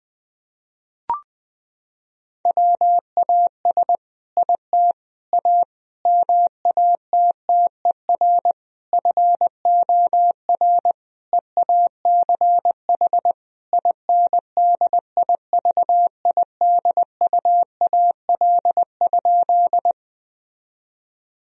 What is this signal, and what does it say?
Morse code, heard as a keyed tone, about 20 wpm, 700 Hz. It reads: WAS IT A MATTER FOR EACH INDIVIDUAL?
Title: Sentences from Top 1000 Words - Review Part 4 - 20wpm